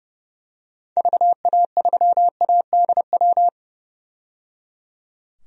Morse code, heard as a keyed tone, about 30 wpm, 700 Hz. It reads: VA3ADW
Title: DXCC-100 Call Signs - 6 Characters - 30wpm